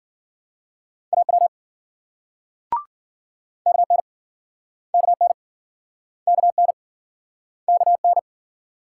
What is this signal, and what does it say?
Morse code, heard as a keyed tone, about 53 wpm, 700 Hz. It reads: AW KN KN KN KN